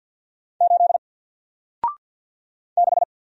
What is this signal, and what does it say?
Morse code, heard as a keyed tone, about 48 wpm, 700 Hz. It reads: 8 6
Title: Single Letter-Number 50wpm